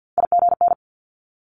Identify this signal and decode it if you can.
Morse code, heard as a keyed tone, about 50 wpm, 700 Hz. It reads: IGN
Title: Multi-letter Phonemes Quick Fire 50wpm